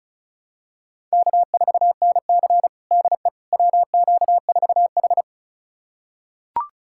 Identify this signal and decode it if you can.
Morse code, heard as a keyed tone, about 36 wpm, 700 Hz. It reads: K4NC DE WQ4H
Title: USA Callsign DE Callsign - 4 Characters - 35wpm